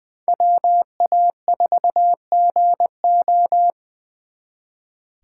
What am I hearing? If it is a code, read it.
Morse code, heard as a keyed tone, about 25 wpm, 700 Hz. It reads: WA4GO
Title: Speed Racing - USA 2x2 Call Signs 20wpm